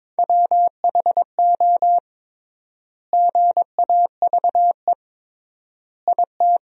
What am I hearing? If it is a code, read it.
Morse code, heard as a keyed tone, about 22 wpm, 700 Hz. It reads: WHO GAVE IT
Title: 3x Word-Spacing - Sentences from Top 500 Words - 22wpm